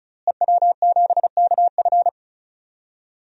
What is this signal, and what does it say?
Morse code, heard as a keyed tone, about 36 wpm, 700 Hz. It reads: EW7KF
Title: DXCC-100 Call Signs - 5 Characters - Repeated 2x - 35wpm